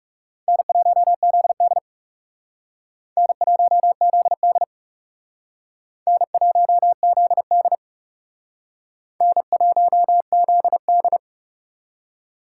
Morse code, heard as a keyed tone, about 39 wpm, 700 Hz. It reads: N1ZD N1ZD N1ZD N1ZD